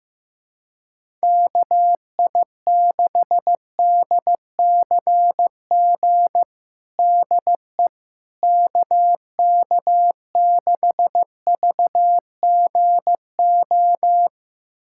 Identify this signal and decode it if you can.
Morse code, heard as a keyed tone, about 15 wpm, 700 Hz. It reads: KI6DCG DE KK6VGO
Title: USA Callsign DE Callsign - 6 Characters - 15wpm